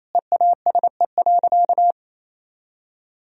Morse code, heard as a keyed tone, about 28 wpm, 700 Hz. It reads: EASE.
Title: Sentences from Top 1000 Words - Review Part 1 - 28wpm